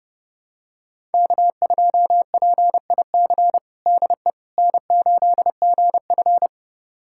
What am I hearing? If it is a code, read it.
Morse code, heard as a keyed tone, about 32 wpm, 700 Hz. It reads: K2PIC DE N8GF
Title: USA Callsign DE Callsign 30wpm